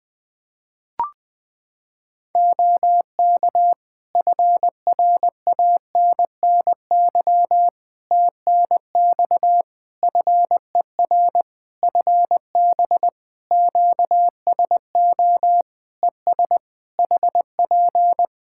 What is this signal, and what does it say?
Morse code, heard as a keyed tone, about 20 wpm, 700 Hz. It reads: OK FRANNY TNX FER FB QSO ES HP
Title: Practice QSO - Exchange 4 - Ending - 20wpm